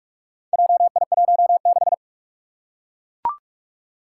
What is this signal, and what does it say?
Morse code, heard as a keyed tone, about 42 wpm, 700 Hz. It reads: JI16
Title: Four Challenging Letter-Number Combinations 45wpm